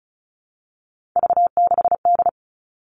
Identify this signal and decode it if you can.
Morse code, heard as a keyed tone, about 34 wpm, 700 Hz. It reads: V6D
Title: All Cut Numbers - 3-Digit 35wpm